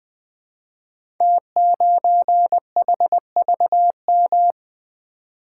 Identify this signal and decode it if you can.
Morse code, heard as a keyed tone, about 20 wpm, 700 Hz. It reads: T9HVM
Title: Code Groups - 5 Letter:Numbers - 20wpm